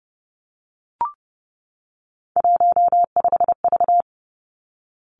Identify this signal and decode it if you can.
Morse code, heard as a keyed tone, about 30 wpm, 700 Hz. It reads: 15V
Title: Three Challenging Letter-Number Combinations 30wpm